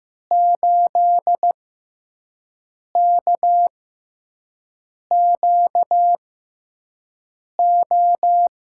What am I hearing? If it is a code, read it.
Morse code, heard as a keyed tone, about 15 wpm, spaced as for 5 wpm, 700 Hz. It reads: 8KQO
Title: USA Call Signs - 6 Characters - 6x Character Spacing - 15wpm